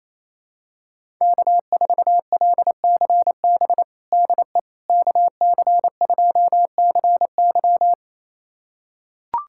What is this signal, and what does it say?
Morse code, heard as a keyed tone, about 28 wpm, 700 Hz. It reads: K4LCB DE KC2CY
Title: USA Callsign DE Callsign - 5 Characters - 28wpm